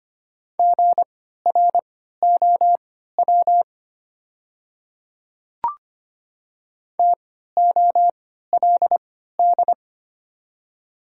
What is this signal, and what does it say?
Morse code, heard as a keyed tone, about 25 wpm, spaced as for 14 wpm, 700 Hz. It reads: GROW TOLD
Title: 3x Character Spacing - Top 500 Words - 25wpm